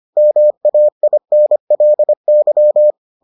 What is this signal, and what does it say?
Morse code, heard as a keyed tone, about 25 wpm, 600 Hz. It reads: MAINLY